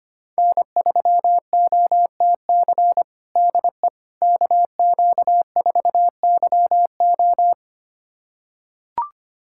Morse code, heard as a keyed tone, about 25 wpm, 700 Hz. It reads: N3OTC DE KQ4YO
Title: USA Callsign DE Callsign - 5 Characters - 25wpm